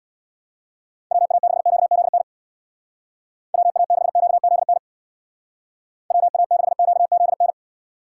Morse code, heard as a keyed tone, about 58 wpm, 700 Hz. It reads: WA6QZN WA6QZN WA6QZN
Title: Speed Racing - USA 2x3 Call Signs 50wpm